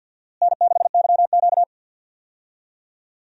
Morse code, heard as a keyed tone, about 47 wpm, 700 Hz. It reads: N6YQ
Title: USA Call Signs - 4 Characters - Repeated 3x - 50wpm